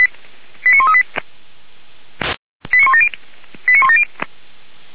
Five-tone selective call (ZVEI), heard as quick runs of five tones, tones of about 70 ms. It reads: 89138 89170 89170
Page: SDR/ZVEI